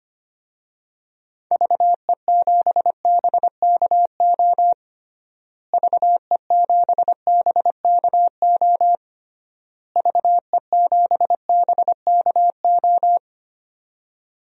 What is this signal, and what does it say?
Morse code, heard as a keyed tone, about 25 wpm, 700 Hz. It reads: VE7BKO VE7BKO VE7BKO
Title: DXCC-100 Call Signs - 6 Characters - Repeated 3x - 25wpm